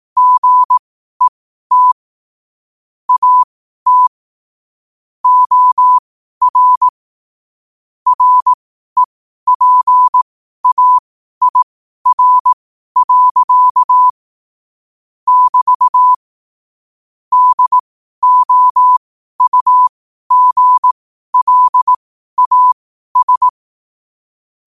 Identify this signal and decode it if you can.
Morse code, heard as a keyed tone, about 18 wpm, spaced as for 12 wpm, 1000 Hz. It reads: GET AT OR REPAIR. <BT> DOUGLAS
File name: qotdfiles / 12wpm